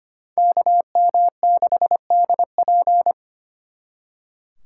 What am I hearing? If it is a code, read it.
Morse code, heard as a keyed tone, about 34 wpm, 700 Hz. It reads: KM6DP